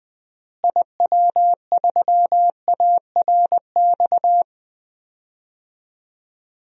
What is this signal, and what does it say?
Morse code, heard as a keyed tone, about 21 wpm, 700 Hz. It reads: IW3ARX
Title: DXCC-100 Call Signs - 6 Characters - Repeated 3x - 20wpm